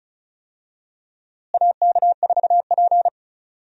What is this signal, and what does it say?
Morse code, heard as a keyed tone, about 34 wpm, 700 Hz. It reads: AK4P
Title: USA Call Signs - 4 Characters - Repeated 2x - 35wpm